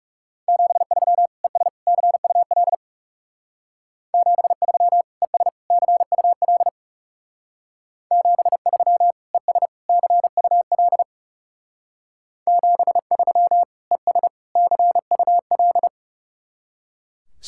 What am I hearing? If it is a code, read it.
Morse code, heard as a keyed tone, about 38 wpm, 700 Hz. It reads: 73 ES CUL 73 ES CUL 73 ES CUL 73 ES CUL